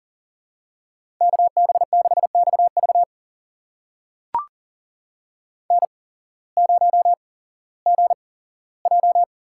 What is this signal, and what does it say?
Morse code, heard as a keyed tone, about 40 wpm, spaced as for 19 wpm, 700 Hz. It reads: KB6XV N0GJ
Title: USA Call Signs - 5 Characters - 8x Character Spacing - 40wpm